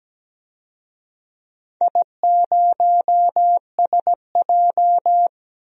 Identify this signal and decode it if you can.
Morse code, heard as a keyed tone, about 17 wpm, 700 Hz. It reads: I0SJ
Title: DXCC-100 Call Signs - 4 Characters - 17wpm